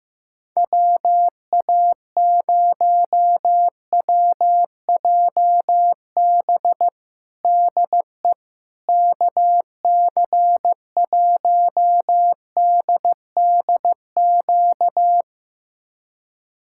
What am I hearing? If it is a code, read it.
Morse code, heard as a keyed tone, about 15 wpm, 700 Hz. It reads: WA0WJB DE KC1DDQ
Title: USA Callsign DE Callsign - 6 Characters - 15wpm